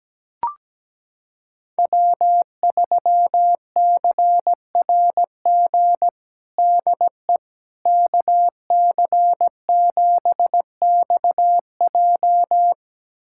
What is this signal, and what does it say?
Morse code, heard as a keyed tone, about 17 wpm, 700 Hz. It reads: W3CRG DE KC7XJ
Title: USA Callsign DE Callsign - 5 Characters - 17wpm